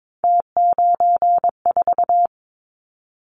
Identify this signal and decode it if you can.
Morse code, heard as a keyed tone, about 22 wpm, 700 Hz. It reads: T94
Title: Callsign Prefixes for Top 50 DXCC - 22wpm